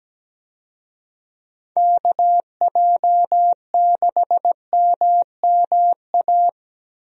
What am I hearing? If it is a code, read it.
Morse code, heard as a keyed tone, about 17 wpm, 700 Hz. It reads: KJ6MMA
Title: USA Call Signs - 6 Characters - Repeated 3x - 17wpm